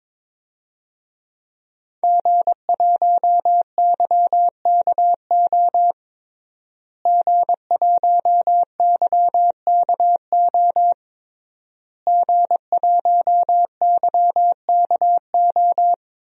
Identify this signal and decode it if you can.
Morse code, heard as a keyed tone, about 22 wpm, 700 Hz. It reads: G1YKO G1YKO G1YKO
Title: DXCC-100 Call Signs - 5 Characters - Repeated 3x - 22wpm